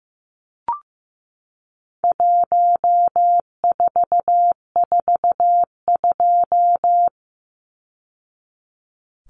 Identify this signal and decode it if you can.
Morse code, heard as a keyed tone, about 15 wpm, 700 Hz. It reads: 1442
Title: Numbers - 4-Digit 15wpm